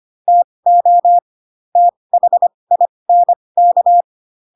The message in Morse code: TO THINK

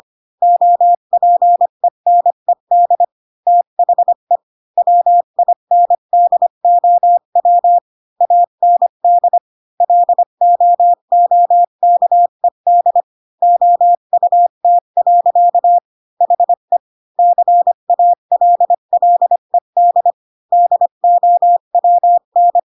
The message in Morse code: OPENED THE WINDOW AND LOOKED OUT. HE CALLED DOWN

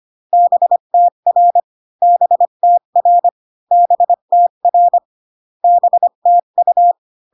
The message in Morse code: BTR BTR BTR BTU

